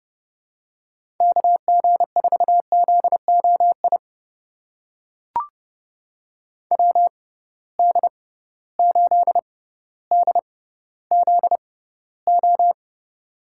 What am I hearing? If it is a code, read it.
Morse code, heard as a keyed tone, about 30 wpm, spaced as for 15 wpm, 700 Hz. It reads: KG4ZOI WD8DZO
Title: USA Call Signs - 6 Characters - 6x Character Spacing - 30wpm